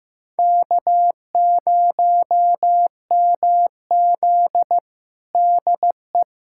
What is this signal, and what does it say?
Morse code, heard as a keyed tone, about 15 wpm, 700 Hz. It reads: K0MZ DE
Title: USA Callsign DE Callsign - 4 Characters - 15wpm